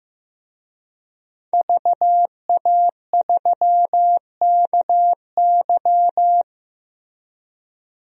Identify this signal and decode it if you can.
Morse code, heard as a keyed tone, about 16 wpm, 700 Hz. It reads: VA3KY
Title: DXCC-100 Call Signs - 5 Characters - Repeated 3x - 15wpm